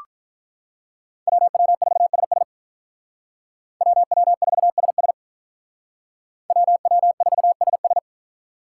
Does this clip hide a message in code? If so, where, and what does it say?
Yes: Morse code from 1.3 to 8.0 s, heard as a keyed tone, about 46 wpm, 700 Hz. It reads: WW4SS WW4SS WW4SS